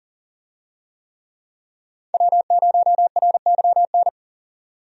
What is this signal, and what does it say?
Morse code, heard as a keyed tone, about 40 wpm, 700 Hz. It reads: W0RYN
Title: USA Call Signs - 5 Characters - 40wpm